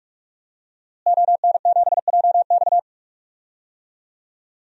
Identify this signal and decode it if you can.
Morse code, heard as a keyed tone, about 46 wpm, 700 Hz. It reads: ON7JX
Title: DXCC-100 Call Signs - 5 Characters - Repeated 3x - 45wpm